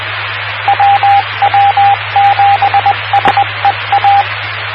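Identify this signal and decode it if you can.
Morse code, heard as a keyed tone, about 20 wpm, 800 Hz, with radio noise. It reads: WW7SEA
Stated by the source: WW7PSR repeater ident on 146.960 MHz, PSRG.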